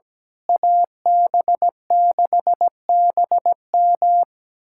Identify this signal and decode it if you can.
Morse code, heard as a keyed tone, about 17 wpm, 700 Hz. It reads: AB6BM
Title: USA Callsign DE Callsign 17wpm